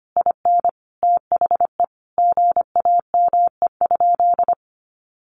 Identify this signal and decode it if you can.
Morse code, heard as a keyed tone, about 25 wpm, 700 Hz. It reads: IN THE GAME?
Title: Sentences from Top 1300 Words 25wpm